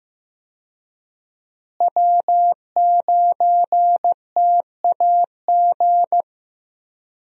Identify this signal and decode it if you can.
Morse code, heard as a keyed tone, about 15 wpm, 700 Hz. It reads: W9TAG
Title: USA Call Signs - 5 Characters - Repeated 3x - 15wpm